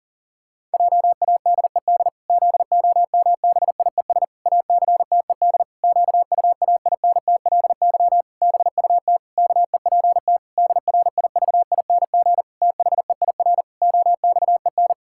Morse code, heard as a keyed tone, about 40 wpm, 700 Hz. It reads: JADED ZOMBIES ACTED QUAINTLY BUT KEPT DRIVING THEIR OXEN
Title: Pangram Challenge v2 - 40wpm